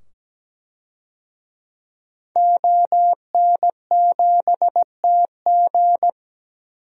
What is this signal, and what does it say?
Morse code, heard as a keyed tone, about 17 wpm, 700 Hz. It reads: ON7TG
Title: DXCC-100 Call Signs - 5 Characters - 17wpm